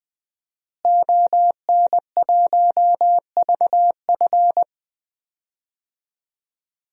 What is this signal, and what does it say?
Morse code, heard as a keyed tone, about 20 wpm, 700 Hz. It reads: ON1VF
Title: DXCC-100 Call Signs - 5 Characters - Repeated 3x - 20wpm